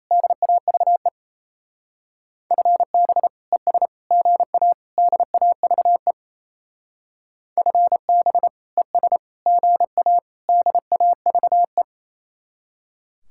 Morse code, heard as a keyed tone, about 37 wpm, 700 Hz. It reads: DAVE FB ES GA DAVE FB ES GA DAVE